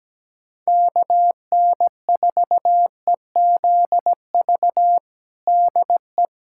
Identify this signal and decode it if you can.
Morse code, heard as a keyed tone, about 17 wpm, 700 Hz. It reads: KN4EZV DE
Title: USA Callsign DE Callsign 17wpm